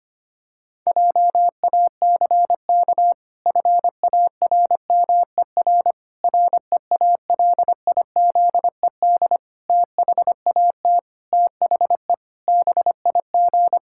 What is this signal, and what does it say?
Morse code, heard as a keyed tone, about 25 wpm, 700 Hz. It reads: JACK FARMER REALIZED THAT THE BIG
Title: Pangram Challenge v2 - 25wpm